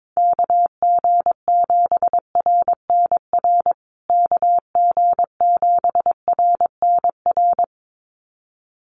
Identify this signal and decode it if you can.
Morse code, heard as a keyed tone, about 22 wpm, 700 Hz. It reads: KG7RNR KG7RNR
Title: USA Call Signs Repeated 2x 22wpm